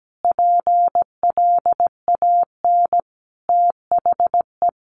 Morse code, heard as a keyed tone, about 17 wpm, 700 Hz. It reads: PLAN THE
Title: Sentences from Top 1400 Words 17wpm